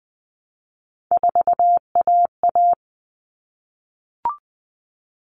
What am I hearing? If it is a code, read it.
Morse code, heard as a keyed tone, about 20 wpm, 700 Hz. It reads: 4AA
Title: All Cut Numbers - 3-Digit 20wpm